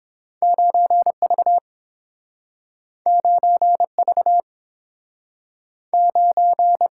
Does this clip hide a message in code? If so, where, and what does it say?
Yes: Morse code from 0.4 to 6.9 s, heard as a keyed tone, about 31 wpm, 700 Hz. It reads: 9V 9V 9